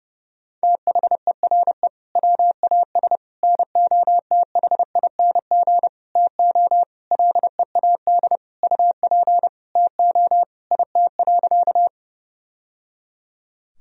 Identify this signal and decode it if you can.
Morse code, heard as a keyed tone, about 30 wpm, 700 Hz. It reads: THERE WAS NOTHING TO LEAD UP TO IT.
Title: Sentences from Top 1000 Words - Review Part 2 - 30wpm